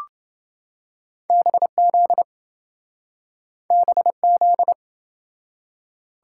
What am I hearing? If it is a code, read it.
Morse code, heard as a keyed tone, about 27 wpm, 700 Hz. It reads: BZ BZ